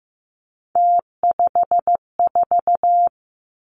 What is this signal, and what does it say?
Morse code, heard as a keyed tone, about 15 wpm, 700 Hz. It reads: T54
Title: Three Challenging Letter-Number Combinations 15wpm